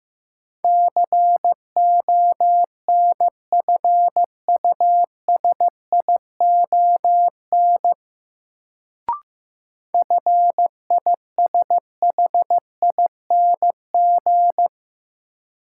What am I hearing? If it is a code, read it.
Morse code, heard as a keyed tone, about 15 wpm, 700 Hz. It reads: CONFUSION FISHING
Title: Sentences from Top 1600 Words - New Word Drill Alt - 15wpm